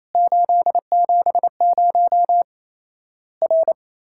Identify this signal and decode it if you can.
Morse code, heard as a keyed tone, about 28 wpm, 700 Hz. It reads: 870 R
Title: Worked All Europe DX Contest - QTC Exchange - 28wpm